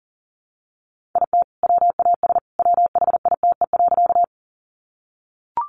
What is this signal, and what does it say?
Morse code, heard as a keyed tone, about 40 wpm, 700 Hz. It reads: IT WAS WHITE.
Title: Sentences from Top 300 Words 40wpm